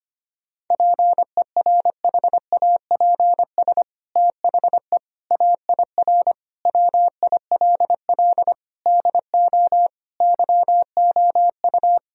Morse code, heard as a keyed tone, about 25 wpm, 700 Hz. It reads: PERHAPS THE AIR WILL DO YOU
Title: Sentences from Top 1000 Words - Review Part 1 - 25wpm